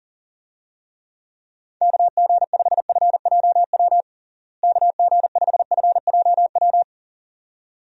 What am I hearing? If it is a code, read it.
Morse code, heard as a keyed tone, about 40 wpm, 700 Hz. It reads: KG5FJW KG5FJW
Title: USA Call Signs - 6 Characters - Repeated 2x - 40wpm